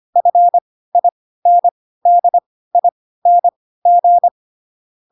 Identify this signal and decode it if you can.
Morse code, heard as a keyed tone, about 25 wpm, spaced as for 16 wpm, 700 Hz. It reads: FINDING